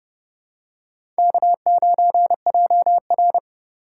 Morse code, heard as a keyed tone, about 30 wpm, 700 Hz. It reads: K9JR
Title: USA Call Signs - 8x Character Spacing - 30wpm